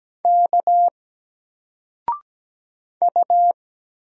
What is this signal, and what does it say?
Morse code, heard as a keyed tone, about 17 wpm, 700 Hz. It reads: K U
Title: Single Letter-Number 17wpm